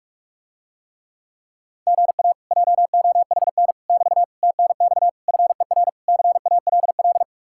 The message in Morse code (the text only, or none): GA JOHN <BT> TNX FER CALL